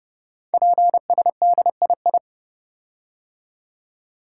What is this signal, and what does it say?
Morse code, heard as a keyed tone, about 30 wpm, 700 Hz. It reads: PSDII
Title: Code Groups - 5 Letter:Numbers - 30wpm